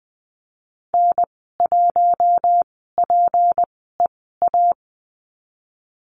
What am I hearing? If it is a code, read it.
Morse code, heard as a keyed tone, about 20 wpm, spaced as for 14 wpm, 700 Hz. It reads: N1PEA
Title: 179 Lesson 29g - Slash - Call Signs 20wpm